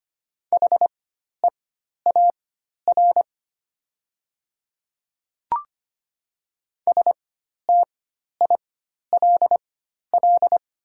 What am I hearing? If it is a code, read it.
Morse code, heard as a keyed tone, about 25 wpm, spaced as for 12 wpm, 700 Hz. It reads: HEAR STILL
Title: 4x Character Spacing - Top 300 Words - 25wpm